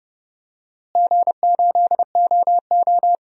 Morse code, heard as a keyed tone, about 30 wpm, 700 Hz. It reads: G8OO
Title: DXCC-100 Call Signs - 4 Characters - Repeated 2x - 30wpm